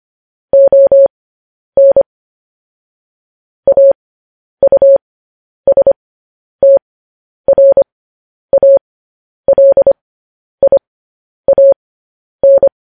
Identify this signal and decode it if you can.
Morse code, heard as a keyed tone, about 25 wpm, spaced as for 10 wpm, 550 Hz. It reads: ON AUSTRALIAN